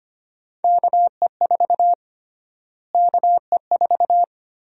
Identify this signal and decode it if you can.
Morse code, heard as a keyed tone, about 25 wpm, 700 Hz. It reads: KE4 KE4
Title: USA Call Sign Prefixes - Repeated 2x - 25wpm